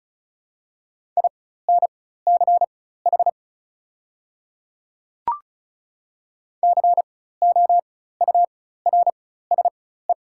Morse code, heard as a keyed tone, about 36 wpm, spaced as for 16 wpm, 690 Hz. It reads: INCH COURSE
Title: 4x Character Spacing - Top 500 Words - 35wpm